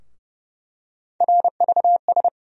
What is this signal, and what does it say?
Morse code, heard as a keyed tone, about 30 wpm, 700 Hz. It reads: RVS
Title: USA Call Sign Postfixes - 3 Letters 30wpm